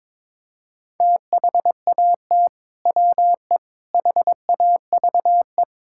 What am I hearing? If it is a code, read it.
Morse code, heard as a keyed tone, about 22 wpm, 700 Hz. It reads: THAT WE HAVE
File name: practice / 22wpm / Sentences with 3-Word Phrases - 22wpm